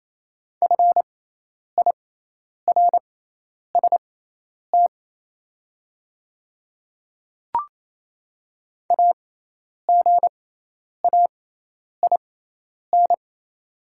Morse code, heard as a keyed tone, about 27 wpm, spaced as for 10 wpm, 700 Hz. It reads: FIRST AGAIN
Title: 6x Character Spacing - Top 200 Words - 28wpm